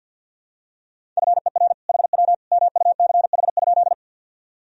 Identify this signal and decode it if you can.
Morse code, heard as a keyed tone, about 50 wpm, 700 Hz. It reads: FER HW MUCH?
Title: Sentences from Top 200 Words and Abbreviations 50wpm